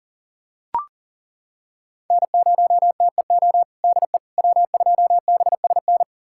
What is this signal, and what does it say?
Morse code, heard as a keyed tone, about 40 wpm, 700 Hz. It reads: N0TEO DE W2BSN
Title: USA Callsign DE Callsign - 5 Characters - 40wpm